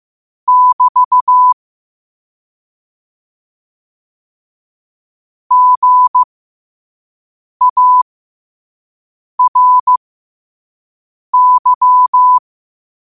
Morse code, heard as a keyed tone, about 15 wpm, spaced as for 5 wpm, 1000 Hz. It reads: <BT> GARY
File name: qotdfiles / 5wpm